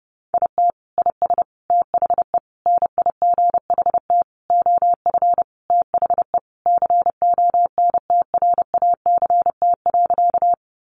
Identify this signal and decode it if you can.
Morse code, heard as a keyed tone, about 30 wpm, 700 Hz. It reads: IT IS THE NIGHT OF THE CONTRACT.
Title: Sentences from Top 1400 Words 30wpm